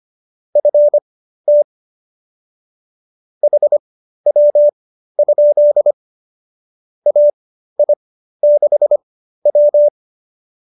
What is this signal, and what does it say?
Morse code, heard as a keyed tone, about 25 wpm, spaced as for 13 wpm, 590 Hz. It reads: FT HW? AI6W